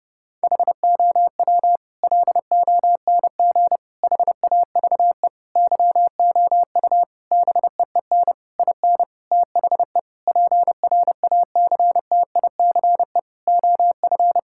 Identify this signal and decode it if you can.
Morse code, heard as a keyed tone, about 30 wpm, 700 Hz. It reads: HOW LONG HAVE YOU BEEN IN THE PRACTICE OF
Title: Sentences from Top 1000 Words - Review Part 6 - 30wpm